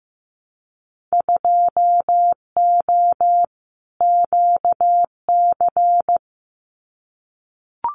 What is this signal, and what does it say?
Morse code, heard as a keyed tone, about 15 wpm, 700 Hz. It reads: 2O QC
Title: Winter Field Day Exchanges 15wpm